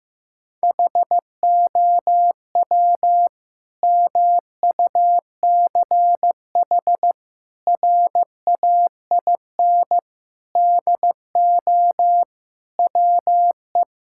Morse code, heard as a keyed tone, about 16 wpm, 700 Hz. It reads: HOW MUCH RAIN DO WE
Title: Sentences from Top 1000 Words - Review Part 6 - 15wpm